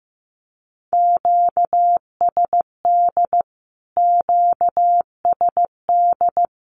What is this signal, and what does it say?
Morse code, heard as a keyed tone, about 15 wpm, 700 Hz. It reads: QSD QSD
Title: Q-Codes Repeated 3x 15wpm